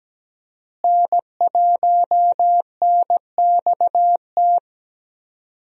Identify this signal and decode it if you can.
Morse code, heard as a keyed tone, about 17 wpm, 700 Hz. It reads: N1NXT
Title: USA Call Signs - 5 Characters - Repeated 2x - 17wpm